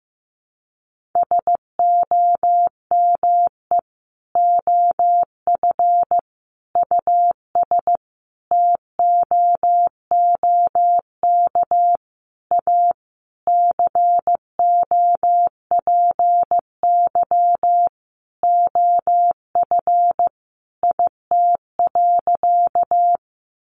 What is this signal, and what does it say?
Morse code, heard as a keyed tone, about 15 wpm, 700 Hz. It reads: SOME OF US TOOK A COPY OF IT.
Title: Sentences from Top 1400 Words 15wpm